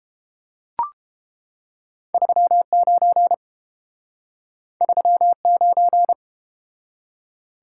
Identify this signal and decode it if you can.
Morse code, heard as a keyed tone, about 30 wpm, 700 Hz. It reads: 39 39